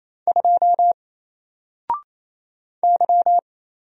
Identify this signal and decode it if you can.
Morse code, heard as a keyed tone, about 27 wpm, 700 Hz. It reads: 2 Y
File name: Single-Letter-Number-28wpm